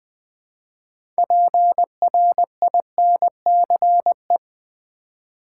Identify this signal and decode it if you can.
Morse code, heard as a keyed tone, about 20 wpm, 700 Hz. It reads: PRINCE
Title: 6 Letter Words - First 3 Letters Given - 20wpm